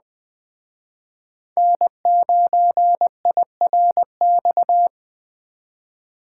Morse code, heard as a keyed tone, about 21 wpm, 700 Hz. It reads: N9IRX
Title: Speed Racing - USA 1x3 Call Signs 17wpm